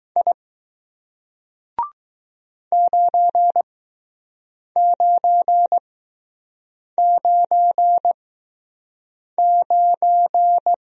Morse code, heard as a keyed tone, about 20 wpm, spaced as for 6 wpm, 700 Hz. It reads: I 9999